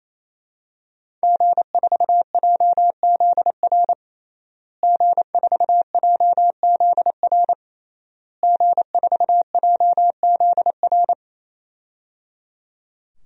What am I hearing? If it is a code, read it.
Morse code, heard as a keyed tone, about 28 wpm, 700 Hz. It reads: G4JZR G4JZR G4JZR
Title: DXCC-100 Call Signs - 5 Characters - Repeated 3x - 28wpm